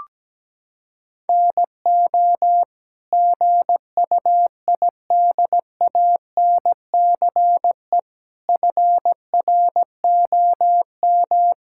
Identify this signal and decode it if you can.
Morse code, heard as a keyed tone, about 17 wpm, 700 Hz. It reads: NO GUIDANCE FROM
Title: Sentences from Top 1600 Words 17wpm